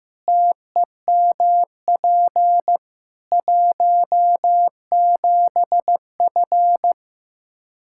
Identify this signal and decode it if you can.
Morse code, heard as a keyed tone, about 15 wpm, 700 Hz. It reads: TEMP 17F
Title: K1USN SST QSO Practice 15wpm